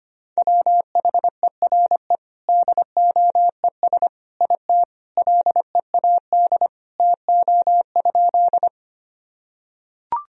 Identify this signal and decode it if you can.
Morse code, heard as a keyed tone, about 25 wpm, 700 Hz. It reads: WHERE DOES IT LEAD TO?
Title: Sentences from Top 1000 Words - Review Part 2 - 25wpm